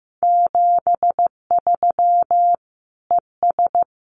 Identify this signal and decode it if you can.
Morse code, heard as a keyed tone, about 15 wpm, 700 Hz. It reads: 73 ES
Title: Common QSO Phrases Encore 15wpm